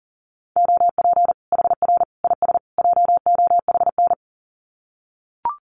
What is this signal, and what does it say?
Morse code, heard as a keyed tone, about 40 wpm, 700 Hz. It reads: OP HR IS JOHN
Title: Common QSO Phrases 40wpm